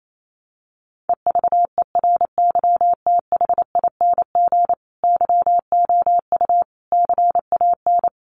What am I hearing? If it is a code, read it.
Morse code, heard as a keyed tone, about 28 wpm, 700 Hz. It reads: EVERYTHING YOU CAN
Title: Sentences from Top 1400 Words 28wpm